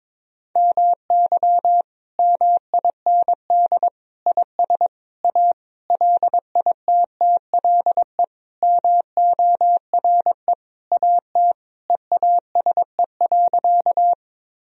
Morse code, heard as a keyed tone, about 22 wpm, 700 Hz. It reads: MY MIND IS A LITTLE MORE AT EASE.
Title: Sentences from Top 1000 Words - Review Part 1 - 22wpm